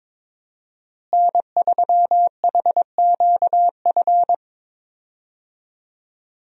Morse code, heard as a keyed tone, about 22 wpm, 700 Hz. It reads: N3HQF
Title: USA Call Signs - 5 Characters - Repeated 2x - 22wpm